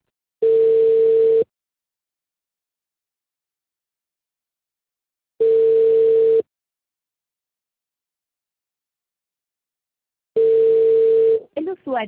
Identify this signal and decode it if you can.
Ringback tone (China, 450 Hz), with speech.